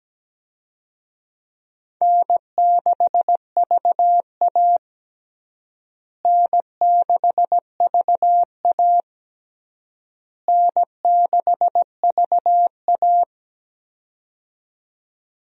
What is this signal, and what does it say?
Morse code, heard as a keyed tone, about 17 wpm, 700 Hz. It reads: N6VA N6VA N6VA
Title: USA Call Signs - 4 Characters - Repeated 3x - 17wpm